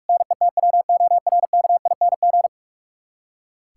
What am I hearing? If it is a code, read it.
Morse code, heard as a keyed tone, about 46 wpm, 700 Hz. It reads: NETWORKING
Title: K1USN SST QSO Practice 45wpm